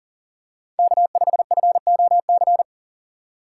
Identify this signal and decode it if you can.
Morse code, heard as a keyed tone, about 40 wpm, 700 Hz. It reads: K5FOC
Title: USA Call Signs - 5 Characters - 2x Character Spacing - 40wpm